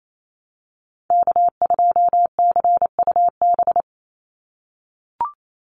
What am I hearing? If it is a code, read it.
Morse code, heard as a keyed tone, about 28 wpm, 700 Hz. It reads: K2CUB
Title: Call Signs - 28wpm